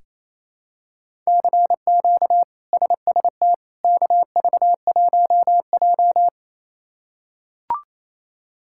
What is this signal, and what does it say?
Morse code, heard as a keyed tone, about 28 wpm, 700 Hz. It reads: CQ SST KV1J
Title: K1USN SST QSO Practice - 28wpm